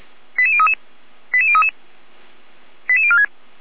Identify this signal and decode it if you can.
Five-tone selective call (ZVEI), heard as quick runs of five tones, tones of about 70 ms. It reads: 88033 88033 88046